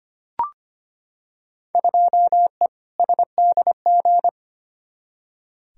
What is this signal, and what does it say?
Morse code, heard as a keyed tone, about 25 wpm, 700 Hz. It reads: 2E SDG
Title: Field Day Exchanges - 25wpm